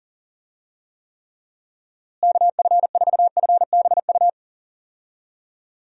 Morse code, heard as a keyed tone, about 40 wpm, 700 Hz. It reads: KF4FBU
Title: USA Call Signs - 6 Characters - Repeated 2x - 40wpm